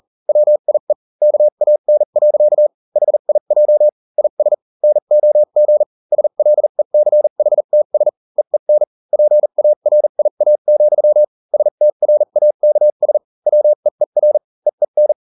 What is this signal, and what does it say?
Morse code, heard as a keyed tone, about 40 wpm, spaced as for 35 wpm, 600 Hz. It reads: WIE KAN. HIJ IS NOG SLECHTS EEN PARIA, STRAKS WEER EEN